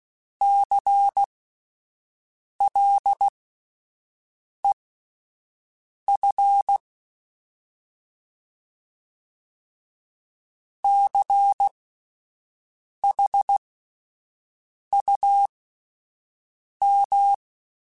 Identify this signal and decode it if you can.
Morse code, heard as a keyed tone, about 16 wpm, spaced as for 5 wpm, 800 Hz. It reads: CLEF CHUM
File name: Alphabetical / Alphabetical mp3s / Cs @5